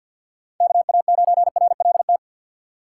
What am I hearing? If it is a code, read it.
Morse code, heard as a keyed tone, about 53 wpm, 700 Hz. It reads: KA9RLT
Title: USA Call Signs - 6 Characters - 6x Character Spacing - 50wpm